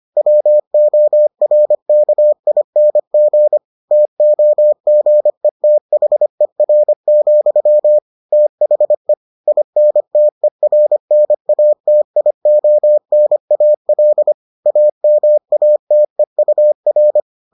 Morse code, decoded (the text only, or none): WORKING TOGETHER, THE INTERNATIONAL AMATEUR